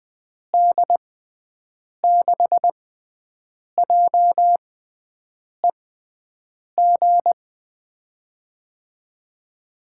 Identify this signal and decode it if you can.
Morse code, heard as a keyed tone, about 20 wpm, spaced as for 7 wpm, 700 Hz. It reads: D6JEG
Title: USA Call Signs - 6 Characters - 6x Character Spacing - 20wpm